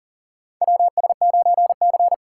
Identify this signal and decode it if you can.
Morse code, heard as a keyed tone, about 40 wpm, 700 Hz. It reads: WS9C
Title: USA Call Signs - 4 Characters - 3x Character Spacing - 40wpm